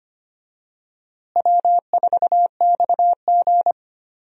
Morse code, heard as a keyed tone, about 25 wpm, 700 Hz. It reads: W4XG
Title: USA Call Signs - 4 Characters - Repeated 2x - 25wpm